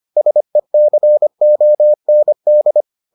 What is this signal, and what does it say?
Morse code, heard as a keyed tone, about 25 wpm, 600 Hz. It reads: SECOND